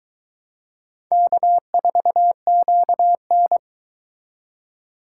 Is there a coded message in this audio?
Yes: Morse code from 1.1 to 3.6 s, heard as a keyed tone, about 23 wpm, 700 Hz. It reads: K4QN